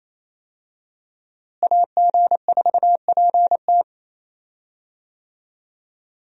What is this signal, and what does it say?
Morse code, heard as a keyed tone, about 28 wpm, 700 Hz. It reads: AG4PT
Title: USA Call Signs - 5 Characters - Repeated 2x - 28wpm